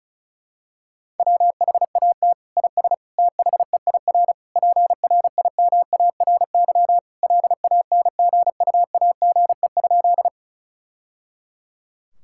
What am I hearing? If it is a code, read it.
Morse code, heard as a keyed tone, about 36 wpm, 700 Hz. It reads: WHAT IS THEIR PRIMARY LANGUAGE?
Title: Sentences from Top 1000 Words - Review Part 5 - 35wpm